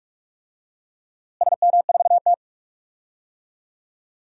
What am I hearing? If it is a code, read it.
Morse code, heard as a keyed tone, about 46 wpm, 700 Hz. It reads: SM4T
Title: DXCC-100 Call Signs - 4 Characters - Repeated 3x - 45wpm